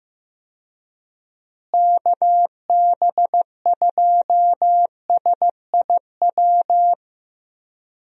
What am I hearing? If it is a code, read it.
Morse code, heard as a keyed tone, about 15 wpm, 700 Hz. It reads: KB2SIW
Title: USA Call Signs - 6 Characters - Repeated 3x - 15wpm